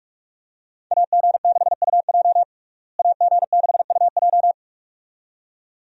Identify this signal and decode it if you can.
Morse code, heard as a keyed tone, about 46 wpm, 700 Hz. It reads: AG6UJ AG6UJ
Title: USA Call Signs - 5 Characters - Repeated 2x - 45wpm